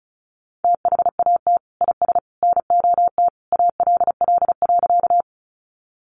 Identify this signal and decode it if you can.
Morse code, heard as a keyed tone, about 34 wpm, 700 Hz. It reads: THAT IS NOT ALL.
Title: Sentences from Top 600 Words 35wpm